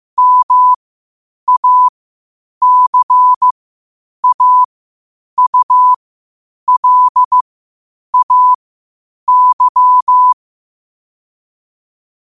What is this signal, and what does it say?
Morse code, heard as a keyed tone, about 15 wpm, spaced as for 8 wpm, 1000 Hz. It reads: MACAULAY
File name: qotdfiles / 8wpm